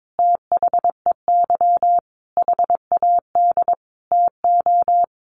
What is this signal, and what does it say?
Morse code, heard as a keyed tone, about 22 wpm, 700 Hz. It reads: THEY HAD TO
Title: Sentences from Top 1400 Words 22wpm